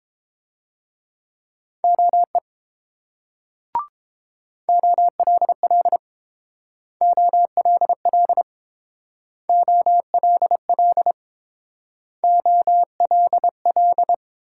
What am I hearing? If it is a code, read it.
Morse code, heard as a keyed tone, about 28 wpm, 700 Hz. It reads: OE OLL OLL OLL OLL